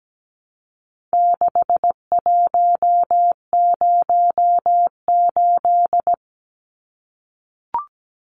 Morse code, heard as a keyed tone, about 17 wpm, 700 Hz. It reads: 6108
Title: Numbers - 4-Digit 17wpm